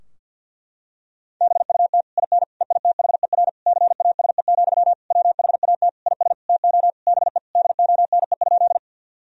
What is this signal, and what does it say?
Morse code, heard as a keyed tone, about 50 wpm, 700 Hz. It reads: BUT IN EITHER CASE, WHAT IS TO BE DONE?
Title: Sentences from Top 1000 Words - Review Part 4 - 50wpm